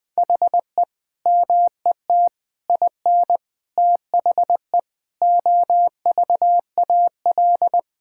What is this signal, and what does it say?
Morse code, heard as a keyed tone, about 20 wpm, 700 Hz. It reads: HE MET IN THE OVAL
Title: Sentences with 8-Word Phrases - 20wpm